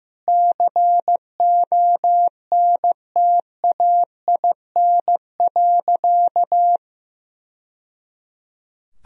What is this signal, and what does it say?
Morse code, heard as a keyed tone, about 15 wpm, 700 Hz. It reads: CONTAIN.
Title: Sentences from Top 1000 Words - Review Part 2 - 15wpm